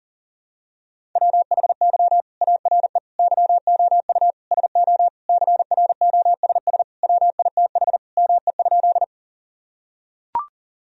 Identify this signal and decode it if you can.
Morse code, heard as a keyed tone, about 40 wpm, 700 Hz. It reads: WHY ARE YOU SO CROSS WITH ME?
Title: Sentences from Top 1000 Words - Review Part 1 - 40wpm